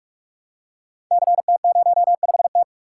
Morse code, heard as a keyed tone, about 44 wpm, 700 Hz. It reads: CT05T
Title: Code Groups - 5 Letter:Numbers - 45wpm